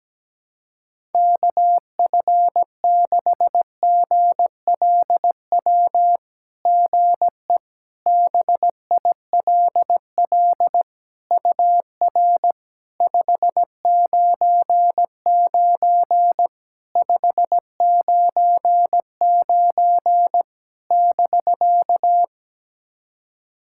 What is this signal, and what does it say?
Morse code, heard as a keyed tone, about 17 wpm, 700 Hz. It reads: KF6GLW GE BILL UR 599 599 <BK>
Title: SOTA QSO Practice 17wpm